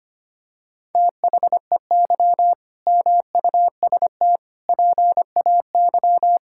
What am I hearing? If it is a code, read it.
Morse code, heard as a keyed tone, about 25 wpm, 700 Hz. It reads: THEY MUST PAY
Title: Sentences from Top 1000 Words - Review Part 6 - 25wpm